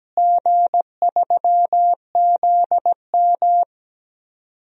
Morse code, heard as a keyed tone, about 17 wpm, 700 Hz. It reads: G3ZM
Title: DXCC-100 Call Signs - 4 Characters - Repeated 3x - 17wpm